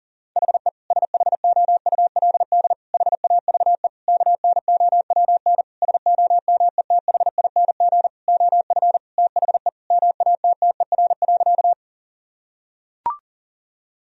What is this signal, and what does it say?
Morse code, heard as a keyed tone, about 40 wpm, 700 Hz. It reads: HE SHOULD HAVE KNOWN SOMETHING OF THE MATTER.
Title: Sentences from Top 1000 Words - Review Part 3 - 40wpm